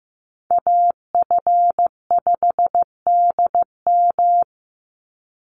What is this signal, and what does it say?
Morse code, heard as a keyed tone, about 15 wpm, 700 Hz. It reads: AF5DM
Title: Call Signs Repeated 2x - 15wpm